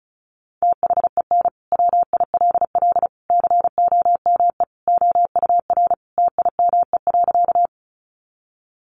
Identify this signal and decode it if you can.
Morse code, heard as a keyed tone, about 34 wpm, 700 Hz. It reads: THEN WILL COME OUR TIME.
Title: Sentences from Top 200 Words 35wpm